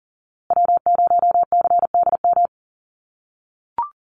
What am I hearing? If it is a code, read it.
Morse code, heard as a keyed tone, about 40 wpm, 700 Hz. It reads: W0CDM
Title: USA Call Signs Repeated 3x 40wpm